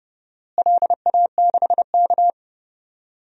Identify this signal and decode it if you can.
Morse code, heard as a keyed tone, about 32 wpm, 700 Hz. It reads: LA6K
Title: DXCC-100 Call Signs - 4 Characters - Repeated 2x - 30wpm